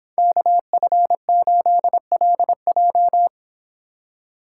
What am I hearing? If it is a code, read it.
Morse code, heard as a keyed tone, about 26 wpm, 700 Hz. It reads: KF8LJ
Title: Speed Racing - USA 2x2 Call Signs 22wpm